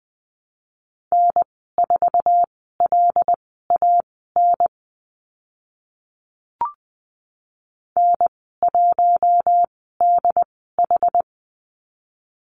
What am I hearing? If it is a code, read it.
Morse code, heard as a keyed tone, about 20 wpm, spaced as for 15 wpm, 700 Hz. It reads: N4LAN N1DH
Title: 088 Lesson 15g - Number 5 - Call Signs 20wpm